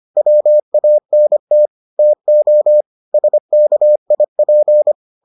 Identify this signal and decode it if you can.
Morse code, heard as a keyed tone, about 25 wpm, 600 Hz. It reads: WANT TO SKIP